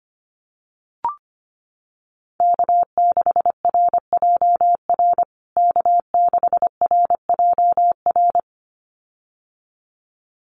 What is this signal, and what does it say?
Morse code, heard as a keyed tone, about 25 wpm, 700 Hz. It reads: K6RJR K6RJR
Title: USA Call Signs Repeated 2x 25wpm